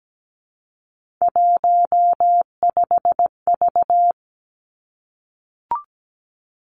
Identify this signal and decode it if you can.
Morse code, heard as a keyed tone, about 17 wpm, 700 Hz. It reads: 15V
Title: Three Challenging Letter-Number Combinations 17wpm